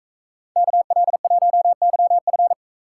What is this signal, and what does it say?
Morse code, heard as a keyed tone, about 44 wpm, 700 Hz. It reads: KL1YF